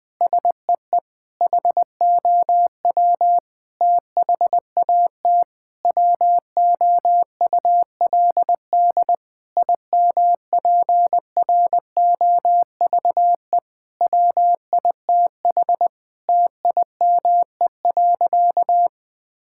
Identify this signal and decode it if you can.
Morse code, heard as a keyed tone, about 21 wpm, 700 Hz. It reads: SEE HOW THAT WOULD IMPROVE WITH TIME.
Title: Sentences from Top 1000 Words - Review Part 5 - 20wpm